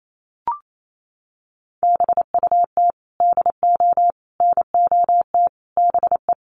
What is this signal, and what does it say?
Morse code, heard as a keyed tone, about 28 wpm, 700 Hz. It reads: BUT DO NOT BE
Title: Sentences from Top 1500 Words 28wpm